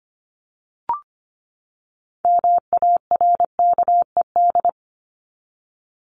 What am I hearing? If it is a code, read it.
Morse code, heard as a keyed tone, about 25 wpm, 700 Hz. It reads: MARKED
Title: ED Suffix - Top 100 Words with Base Words 25wpm